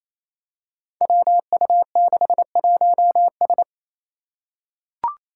Morse code, heard as a keyed tone, about 28 wpm, 700 Hz. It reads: WU61S
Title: Five Challenging Letter-Number Combinations 28wpm